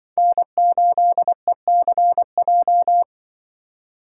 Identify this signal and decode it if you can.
Morse code, heard as a keyed tone, about 24 wpm, 700 Hz. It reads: N8ECJ
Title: Speed Racing - USA 1x3 Call Signs 20wpm